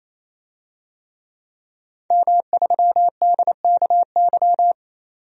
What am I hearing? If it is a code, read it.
Morse code, heard as a keyed tone, about 28 wpm, 700 Hz. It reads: M3DKY
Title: DXCC-100 Call Signs - 5 Characters - 28wpm